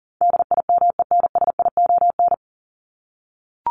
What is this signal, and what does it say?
Morse code, heard as a keyed tone, about 40 wpm, 700 Hz. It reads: DIMENSION
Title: Top 1501-1750 Words 40wpm